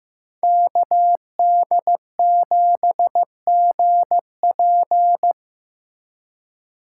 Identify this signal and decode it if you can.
Morse code, heard as a keyed tone, about 18 wpm, 700 Hz. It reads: KD7GP
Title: Speed Racing - USA 2x2 Call Signs 15wpm